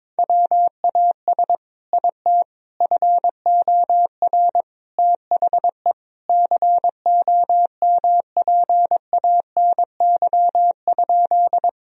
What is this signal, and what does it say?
Morse code, heard as a keyed tone, about 22 wpm, 700 Hz. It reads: WAS IT FOR THE COMPANY?
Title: Sentences from Top 1000 Words - Review Part 3 - 22wpm